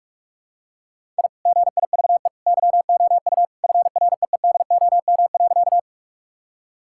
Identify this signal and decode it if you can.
Morse code, heard as a keyed tone, about 45 wpm, 700 Hz. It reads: I GIVE YOU FREEDOM.
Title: Sentences from Top 1000 Words - Review Part 5 - 45wpm